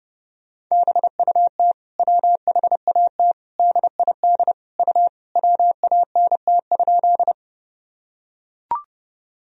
Morse code, heard as a keyed tone, about 30 wpm, 700 Hz. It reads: BUT WHAT DID U WANT?
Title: Sentences from Top 200 Words and Abbreviations 30wpm